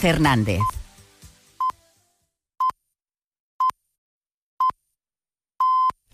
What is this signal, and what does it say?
Radio time signal (time pips), with speech. It is heard from 0.6 to 5.9 s.